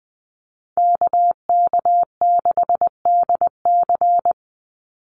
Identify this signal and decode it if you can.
Morse code, heard as a keyed tone, about 20 wpm, 700 Hz. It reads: KK6DC
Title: USA Call Signs 20wpm